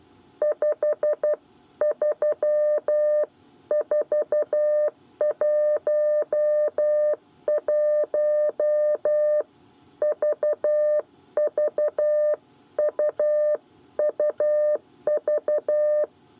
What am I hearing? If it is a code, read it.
Morse code, heard as a keyed tone, about 12 wpm, 600 Hz. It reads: 53411 VVUUV
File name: 8 bis lezione cw 1